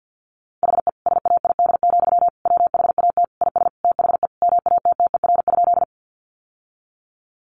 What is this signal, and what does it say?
Morse code, heard as a keyed tone, about 50 wpm, 700 Hz. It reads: HE SAID, WHAT IS THE MATTER?
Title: Sentences from Top 700 Words 50wpm